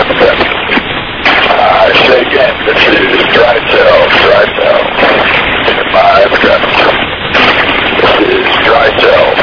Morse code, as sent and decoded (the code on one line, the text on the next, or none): none